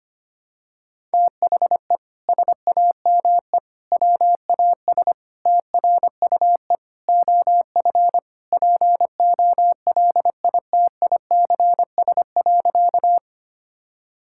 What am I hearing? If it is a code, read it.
Morse code, heard as a keyed tone, about 25 wpm, 700 Hz. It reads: THE SAME WAS TRUE OF POLITICS.
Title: Sentences from Top 1000 Words - Review Part 6 - 25wpm